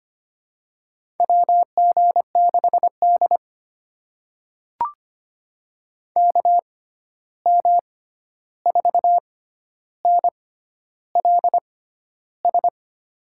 Morse code, heard as a keyed tone, about 25 wpm, spaced as for 11 wpm, 700 Hz. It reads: WG6D KM4NLS
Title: USA Call Signs - 6x Character Spacing - 25wpm